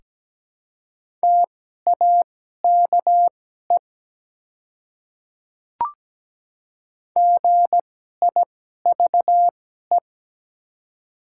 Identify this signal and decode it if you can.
Morse code, heard as a keyed tone, about 17 wpm, spaced as for 12 wpm, 700 Hz. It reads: TAKE GIVE
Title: 2x Character Spacing - Top 300 Words - 17wpm